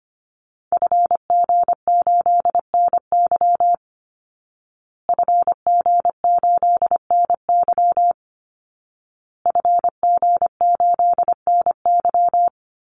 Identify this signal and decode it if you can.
Morse code, heard as a keyed tone, about 25 wpm, 700 Hz. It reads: FG8NY FG8NY FG8NY
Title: Call Signs Repeated 2x - 25wpm